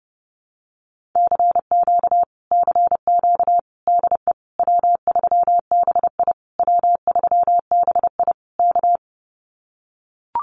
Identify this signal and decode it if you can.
Morse code, heard as a keyed tone, about 30 wpm, 700 Hz. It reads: CQ CQ DE W3BI W3BI K
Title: Common QSO Phrases Encore 30wpm